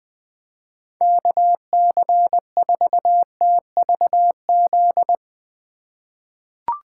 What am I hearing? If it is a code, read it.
Morse code, heard as a keyed tone, about 20 wpm, 700 Hz. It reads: KC4TVZ
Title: 2023 POTA Activator Call Signs - 20wpm